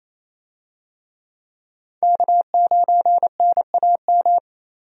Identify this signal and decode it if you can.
Morse code, heard as a keyed tone, about 28 wpm, 700 Hz. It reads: K9NAM
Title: USA Call Signs - 5 Characters - Repeated 3x - 28wpm